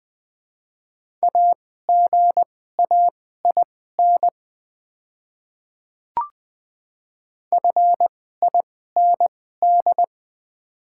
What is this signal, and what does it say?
Morse code, heard as a keyed tone, about 20 wpm, spaced as for 14 wpm, 700 Hz. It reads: AGAIN FIND
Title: 2x Character Spacing - Top 200 Words - 20wpm